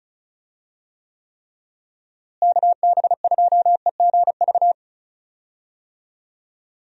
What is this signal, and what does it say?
Morse code, heard as a keyed tone, about 36 wpm, 700 Hz. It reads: KB2EGV
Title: USA Call Signs - 6 Characters - Repeated 3x - 35wpm